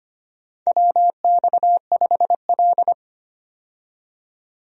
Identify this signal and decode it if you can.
Morse code, heard as a keyed tone, about 25 wpm, 700 Hz. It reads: WX5L
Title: USA Call Signs - 4 Characters - 25wpm